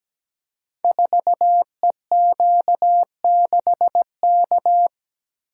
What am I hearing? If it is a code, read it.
Morse code, heard as a keyed tone, about 17 wpm, 700 Hz. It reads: 4EQ6K
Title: Code Groups - 5 Letter:Numbers - 17wpm